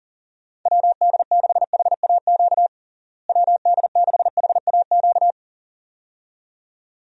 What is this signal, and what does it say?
Morse code, heard as a keyed tone, about 40 wpm, 700 Hz. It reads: WD6HAQ WD6HAQ
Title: USA Call Signs - 6 Characters - Repeated 2x - 40wpm